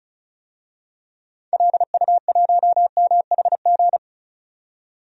Morse code, heard as a keyed tone, about 36 wpm, 700 Hz. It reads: LU1MHG
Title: DXCC-100 Call Signs - 6 Characters - 35wpm